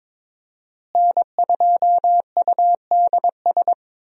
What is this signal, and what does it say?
Morse code, heard as a keyed tone, about 22 wpm, 700 Hz. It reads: N2UDS
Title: USA Call Signs - 5 Characters - Repeated 2x - 22wpm